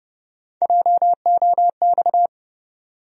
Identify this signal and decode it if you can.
Morse code, heard as a keyed tone, about 30 wpm, 700 Hz. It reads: JOX
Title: USA Call Sign Postfixes - 3 Letters 30wpm